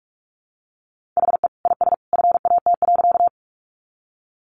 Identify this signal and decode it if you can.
Morse code, heard as a keyed tone, about 44 wpm, 700 Hz. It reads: HE IS FAT.
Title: Easy Phrases 45wpm